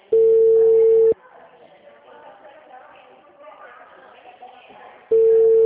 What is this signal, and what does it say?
Ringback tone (China, 450 Hz), with speech.